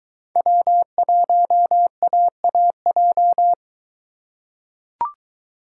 Morse code, heard as a keyed tone, about 22 wpm, 700 Hz. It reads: W1AAJ